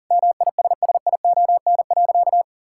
Morse code, heard as a keyed tone, about 40 wpm, 700 Hz. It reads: MISSION.
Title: Sentences from Top 1600 Words 40wpm